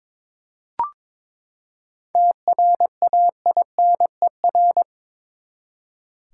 Morse code, heard as a keyed tone, about 22 wpm, 700 Hz. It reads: TRAINER
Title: Seven Letter Words - 22wpm